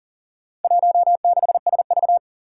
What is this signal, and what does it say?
Morse code, heard as a keyed tone, about 40 wpm, 700 Hz. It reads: 16SV
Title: Four Challenging Letter-Number Combinations 40wpm